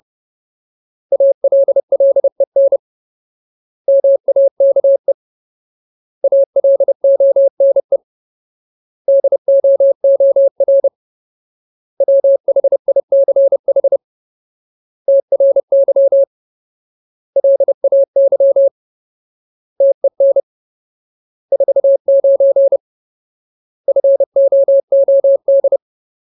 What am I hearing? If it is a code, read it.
Morse code, heard as a keyed tone, about 30 wpm, 550 Hz. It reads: ALLEN MAKE ALONE DOOR WHICH TRY LAY TEN 49 FOOD
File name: Ultimate Word List / 30WPM Ultimate